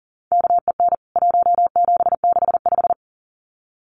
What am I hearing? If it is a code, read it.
Morse code, heard as a keyed tone, about 40 wpm, 700 Hz. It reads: KEN 1765
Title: CWT Exchange 40wpm